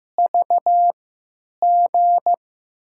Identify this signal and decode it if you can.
Morse code, heard as a keyed tone, about 15 wpm, spaced as for 9 wpm, 700 Hz. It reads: VG
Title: USA Call Signs - 6 Characters - 3x Character Spacing - 15wpm